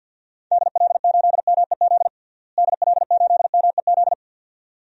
Morse code, heard as a keyed tone, about 51 wpm, 700 Hz. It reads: DL8MEZ DL8MEZ
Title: DXCC-100 Call Signs - 6 Characters - Repeated 2x - 50wpm